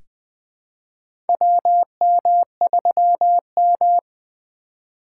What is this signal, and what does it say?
Morse code, heard as a keyed tone, about 20 wpm, 700 Hz. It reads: WM3M
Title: USA Call Signs - 4 Characters - 4x Character Spacing - 20wpm